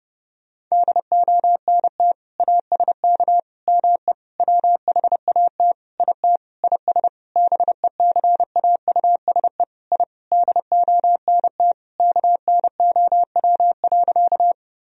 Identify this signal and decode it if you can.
Morse code, heard as a keyed tone, about 30 wpm, 700 Hz. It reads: DONT ASK ME WHAT IT IS BECAUSE I DONT KNOW.
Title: Sentences from Top 1000 Words - Review Part 3 - 30wpm